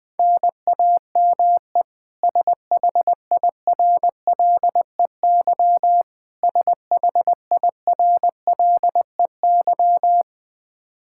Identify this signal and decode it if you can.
Morse code, heard as a keyed tone, about 21 wpm, 700 Hz. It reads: NAME SHIRLEY SHIRLEY
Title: K1USN SST QSO Practice 20wpm